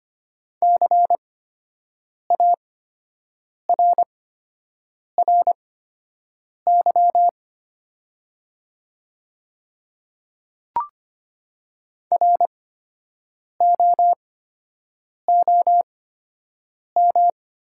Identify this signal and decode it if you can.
Morse code, heard as a keyed tone, about 25 wpm, spaced as for 7 wpm, 700 Hz. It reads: CARRY ROOM
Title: 8x Character Spacing - Top 300 Words - 25wpm